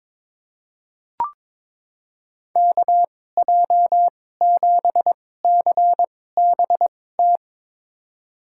Morse code, heard as a keyed tone, about 23 wpm, spaced as for 16 wpm, 700 Hz. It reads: KJ7CBT
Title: USA Call Signs - 6 Characters - 2x Character Spacing - 22wpm